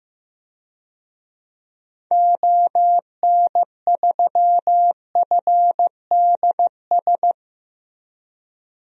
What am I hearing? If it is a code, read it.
Morse code, heard as a keyed tone, about 16 wpm, 700 Hz. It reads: ON3FDS
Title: DXCC-100 Call Signs - 6 Characters - Repeated 2x - 15wpm